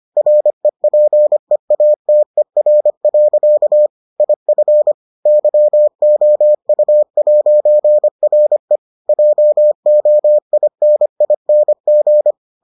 Morse code, heard as a keyed tone, about 25 wpm, 600 Hz. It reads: REPEATER. IF YOU'RE JOINING